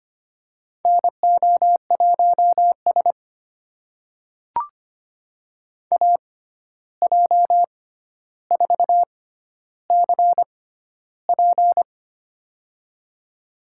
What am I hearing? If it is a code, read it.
Morse code, heard as a keyed tone, about 25 wpm, spaced as for 12 wpm, 700 Hz. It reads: NO1S AJ4CP
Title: USA Call Signs - 6x Character Spacing - 25wpm